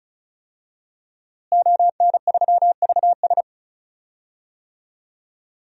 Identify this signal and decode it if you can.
Morse code, heard as a keyed tone, about 36 wpm, 700 Hz. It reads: ON3VS
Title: DXCC-100 Call Signs - 5 Characters - Repeated 3x - 35wpm